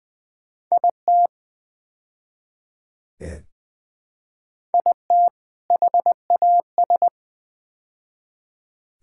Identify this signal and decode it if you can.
Morse code, heard as a keyed tone, about 20 wpm, 700 Hz. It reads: IT IT HAS
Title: Build A Phrase - 5-Word Phrases - 20wpm